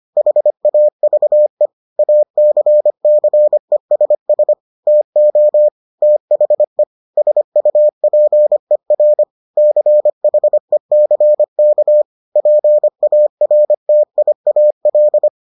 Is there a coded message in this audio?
Yes: Morse code from 0.2 to 15.3 s, heard as a keyed tone, about 25 wpm, 600 Hz. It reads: HAVE ACCESS TO THE SUPER CHECK PARTIAL